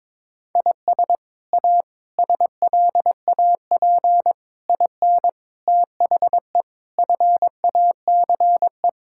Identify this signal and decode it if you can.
Morse code, heard as a keyed tone, about 22 wpm, 700 Hz. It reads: IS A SLAP IN THE FACE
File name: practice / 22wpm / Sets of 8 Words - 22wpm